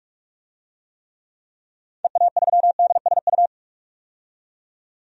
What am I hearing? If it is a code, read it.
Morse code, heard as a keyed tone, about 46 wpm, 700 Hz. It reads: EA3DSU
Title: DXCC-100 Call Signs - 6 Characters - 45wpm